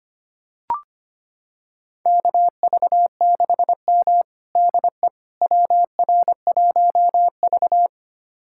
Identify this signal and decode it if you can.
Morse code, heard as a keyed tone, about 25 wpm, 700 Hz. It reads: KV6M DE WR1V
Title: USA Callsign DE Callsign - 4 Characters - 25wpm